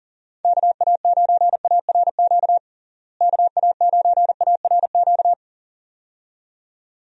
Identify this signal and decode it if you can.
Morse code, heard as a keyed tone, about 40 wpm, 700 Hz. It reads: KA9ARQ KA9ARQ
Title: USA Call Signs - 6 Characters - Repeated 2x - 40wpm